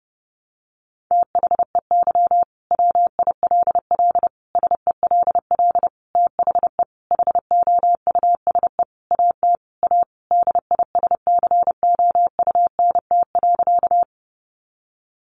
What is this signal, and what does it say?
Morse code, heard as a keyed tone, about 28 wpm, 700 Hz. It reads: THEY WILL SELL THE HOUSE AT A DISCOUNT.
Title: Sentences from Top 1200 Words 30wpm